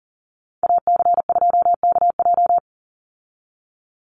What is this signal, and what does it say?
Morse code, heard as a keyed tone, about 39 wpm, 700 Hz. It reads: AC2KJ
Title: Call Signs Repeated 2x - 40wpm